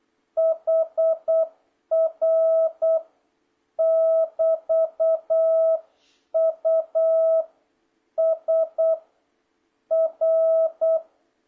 Morse code, heard as a keyed tone, about 8 wpm, 650 Hz. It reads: HR <BT> U S R